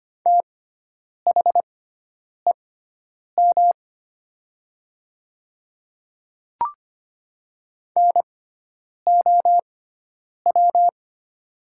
Morse code, heard as a keyed tone, about 25 wpm, spaced as for 9 wpm, 700 Hz. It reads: THEM NOW
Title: 6x Character Spacing - Top 100 Words - 25wpm